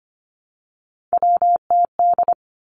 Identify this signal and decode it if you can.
Morse code, heard as a keyed tone, about 25 wpm, 700 Hz. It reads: WTD
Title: Three Challenging Letter-Number Combinations 25wpm